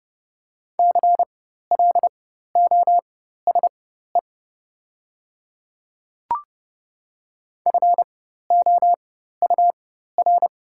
Morse code, heard as a keyed tone, about 30 wpm, spaced as for 14 wpm, 700 Hz. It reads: CLOSE FOUR
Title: 4x Character Spacing - Top 300 Words - 30wpm